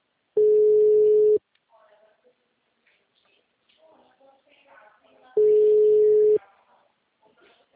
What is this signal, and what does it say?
Ringback tone (Europe, 425 Hz), with speech.
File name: out- -2008- - - .73